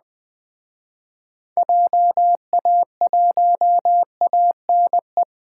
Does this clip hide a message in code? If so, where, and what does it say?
Yes: Morse code from 1.6 to 5.2 s, heard as a keyed tone, about 20 wpm, 700 Hz. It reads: JA1ANE